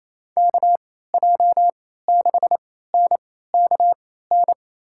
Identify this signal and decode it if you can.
Morse code, heard as a keyed tone, about 28 wpm, spaced as for 16 wpm, 700 Hz. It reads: KJ6NKN
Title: USA Call Signs - 6 Characters - 3x Character Spacing - 28wpm